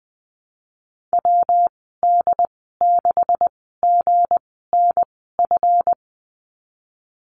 Morse code, heard as a keyed tone, about 20 wpm, spaced as for 14 wpm, 700 Hz. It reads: WD6GNF
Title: 179 Lesson 29g - Slash - Call Signs 20wpm